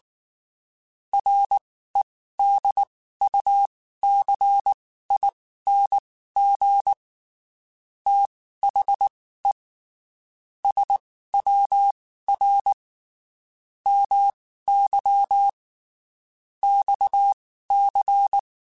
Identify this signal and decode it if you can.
Morse code, heard as a keyed tone, about 19 wpm, spaced as for 14 wpm, 800 Hz. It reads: REDUCING THE SWR MY XC
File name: (spaced) / QSOs Spaced-mp3s / WD0FBT 1 @18 sp